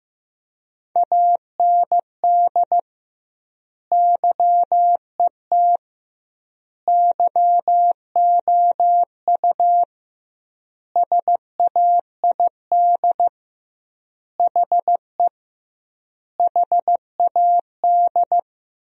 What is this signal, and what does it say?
Morse code, heard as a keyed tone, about 15 wpm, 700 Hz. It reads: AND YET YOU SAID HE HAD
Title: 2x Word-Spacing - Sentences from Top 500 Words - 15wpm